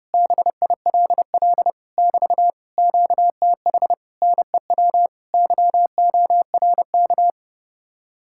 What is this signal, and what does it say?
Morse code, heard as a keyed tone, about 30 wpm, 700 Hz. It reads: BILL <BT> QTH NEW YORK